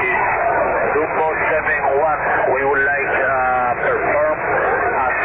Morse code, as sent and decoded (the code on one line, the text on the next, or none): none